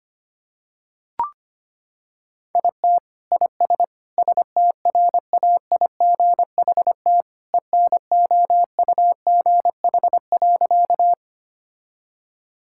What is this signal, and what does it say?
Morse code, heard as a keyed tone, about 25 wpm, 700 Hz. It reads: IT IS STRAIGHT ENOUGH.
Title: Sentences from Top 1000 Words - Review Part 6 - 25wpm